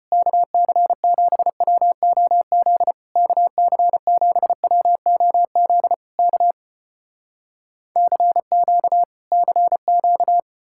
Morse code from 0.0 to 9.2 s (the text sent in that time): KC7WOZ KC7WOZ K CQ